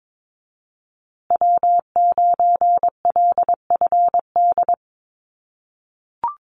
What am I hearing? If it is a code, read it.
Morse code, heard as a keyed tone, about 22 wpm, 700 Hz. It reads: W9LFD
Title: USA Call Signs Repeated 2x 22wpm